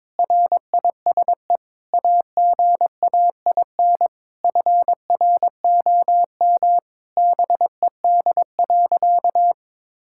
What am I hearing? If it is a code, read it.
Morse code, heard as a keyed tone, about 22 wpm, 700 Hz. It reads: RISE AGAIN FROM BED.
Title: Sentences from Top 1000 Words - Review Part 6 - 22wpm